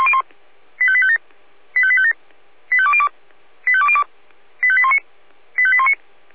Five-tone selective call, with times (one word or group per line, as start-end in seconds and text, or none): 0.8-1.2 85857
1.8-2.1 85857
2.7-3.1 85202
3.7-4.0 85202
4.6-5.0 85819
5.6-5.9 85819